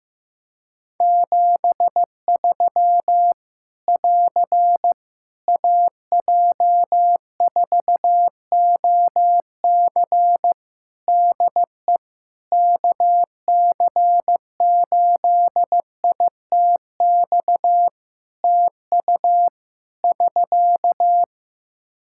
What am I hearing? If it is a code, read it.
Morse code, heard as a keyed tone, about 15 wpm, 700 Hz. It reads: 73 <AR> AJ4OC DE KC8ITX TU <SK>
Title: Practice QSO - Exchange 4 - Ending - 15wpm